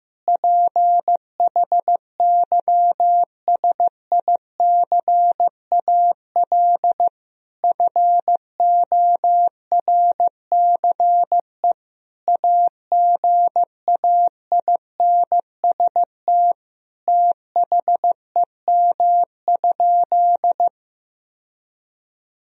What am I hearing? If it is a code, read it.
Morse code, heard as a keyed tone, about 15 wpm, 700 Hz. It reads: PHYSICAL FORCE AGAINST THEM?
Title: Sentences from Top 1000 Words - Review Part 4 - 15wpm